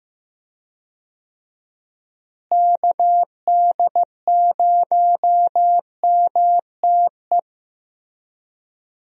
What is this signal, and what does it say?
Morse code, heard as a keyed tone, about 15 wpm, 700 Hz. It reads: KD0MTE
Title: USA Call Signs - 6 Characters - Repeated 2x - 15wpm